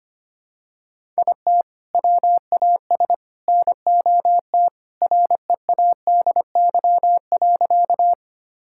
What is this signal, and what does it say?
Morse code, heard as a keyed tone, about 25 wpm, 700 Hz. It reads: IT WAS NOT READY.
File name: practice / 25wpm / Sentences with 2-Word Phrases - 25wpm